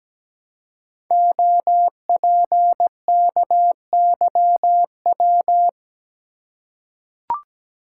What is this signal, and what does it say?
Morse code, heard as a keyed tone, about 17 wpm, 700 Hz. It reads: OPKYW
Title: Code Groups - 5 Letter:Numbers - 17wpm